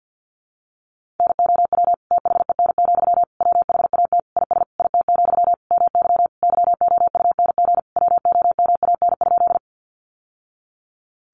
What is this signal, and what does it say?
Morse code, heard as a keyed tone, about 50 wpm, 700 Hz. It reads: NOW THEN, WHAT IS IT, MY YOUNG WOMAN?
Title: Sentences from Top 1100 Words 50wpm